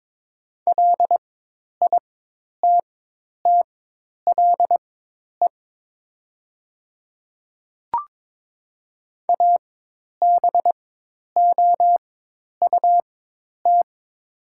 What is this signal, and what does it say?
Morse code, heard as a keyed tone, about 22 wpm, spaced as for 10 wpm, 700 Hz. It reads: LITTLE ABOUT
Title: 4x Character Spacing - Top 200 Words - 22wpm